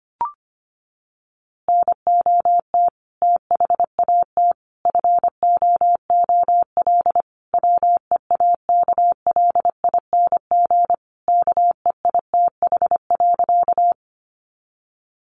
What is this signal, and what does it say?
Morse code, heard as a keyed tone, about 25 wpm, 700 Hz. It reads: NOT THAT FOOL WEAKLING KEITH.
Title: A Sound Of Thunder 25wpm